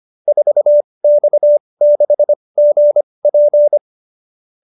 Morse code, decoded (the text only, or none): 4X6GP